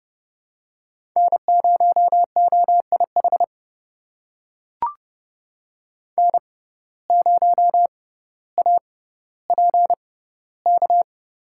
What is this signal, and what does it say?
Morse code, heard as a keyed tone, about 30 wpm, spaced as for 15 wpm, 700 Hz. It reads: N0OIH N0APK
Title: USA Call Signs - 5 Characters - 6x Character Spacing - 30wpm